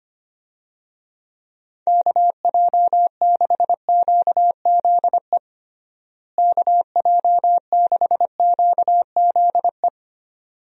Morse code, heard as a keyed tone, about 25 wpm, 700 Hz. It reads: KJ6QZE KJ6QZE
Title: USA Call Signs - 6 Characters - Repeated 2x - 25wpm